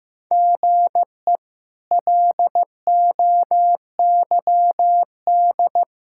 Morse code, heard as a keyed tone, about 15 wpm, 700 Hz. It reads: GE LOYD
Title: K1USN SST QSO Practice - 15wpm